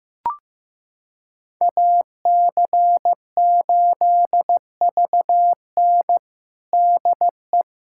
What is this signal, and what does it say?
Morse code, heard as a keyed tone, about 15 wpm, 700 Hz. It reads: AC8VN DE
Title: USA Callsign DE Callsign - 5 Characters - 15wpm